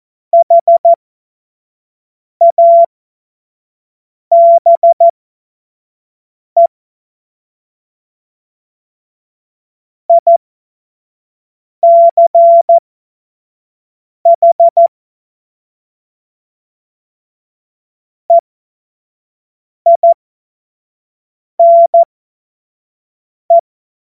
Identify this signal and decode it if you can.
Morse code, heard as a keyed tone, about 14 wpm, spaced as for 5 wpm, 680 Hz. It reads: HABE ICH EINE
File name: Morsekurs / Witze / Witz 27. April 2021, 25 F70